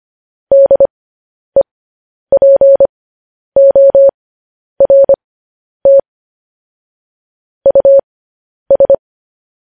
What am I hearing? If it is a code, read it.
Morse code, heard as a keyed tone, about 25 wpm, spaced as for 10 wpm, 550 Hz. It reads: DEPORT US
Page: news headlines in morse code at 10 wpm